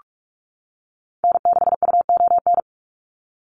Morse code, heard as a keyed tone, about 44 wpm, 690 Hz. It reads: N6UON
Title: USA Call Signs 45wpm